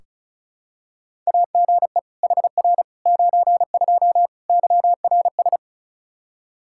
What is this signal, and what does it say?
Morse code, heard as a keyed tone, about 36 wpm, 700 Hz. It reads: AGE HR 92 YRS
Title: Practice QSO - Exchange 3 - More Info - 35wpm